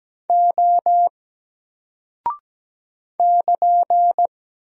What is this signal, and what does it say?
Morse code, heard as a keyed tone, about 17 wpm, 700 Hz. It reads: O <KN>
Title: Single Character (Letter:Number:Punctuation:Prosign) Alt 17wpm